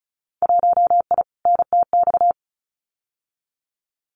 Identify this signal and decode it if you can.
Morse code, heard as a keyed tone, about 34 wpm, 700 Hz. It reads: 1I NTX
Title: Winter Field Day Exchanges Encore 35wpm